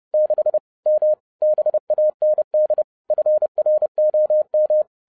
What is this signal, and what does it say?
Morse code, heard as a keyed tone, about 30 wpm, 610 Hz. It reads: 6 M BAND FROM